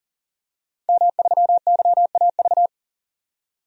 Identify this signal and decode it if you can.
Morse code, heard as a keyed tone, about 40 wpm, 700 Hz. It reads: M3YAV
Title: DXCC-100 Call Signs - 5 Characters - 40wpm